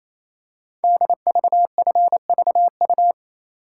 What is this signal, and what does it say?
Morse code, heard as a keyed tone, about 29 wpm, 700 Hz. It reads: DVFVU
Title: Code Groups - 5 Letters - 28wpm